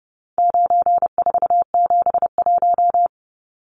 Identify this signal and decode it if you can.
Morse code, heard as a keyed tone, about 30 wpm, 700 Hz. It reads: 9471
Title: Numbers - 4-Digit 30wpm